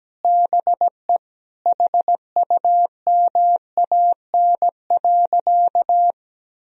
Morse code, heard as a keyed tone, about 17 wpm, 700 Hz. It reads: BE HUMAN.
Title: Sentences from Top 1000 Words - Review Part 3 - 17wpm